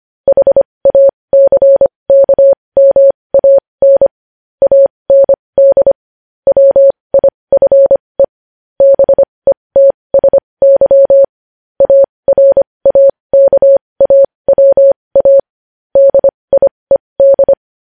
Morse code, heard as a keyed tone, about 25 wpm, spaced as for 20 wpm, 550 Hz. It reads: HACKMAN AND WIFE BETSY ARAKAWA DIED
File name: podcast / News headlines-1x-25x20